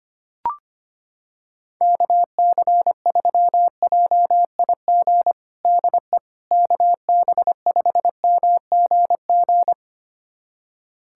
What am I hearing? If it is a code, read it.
Morse code, heard as a keyed tone, about 25 wpm, 700 Hz. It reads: KC3JIG DE KB5MGG
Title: USA Callsign DE Callsign 25wpm